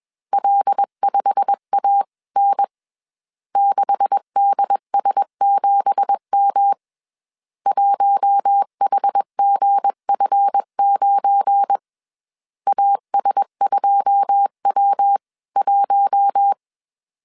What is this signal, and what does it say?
Morse code, heard as a keyed tone, about 21 wpm, 800 Hz. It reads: L5AN 6DS7M 1HGF9 AS2W1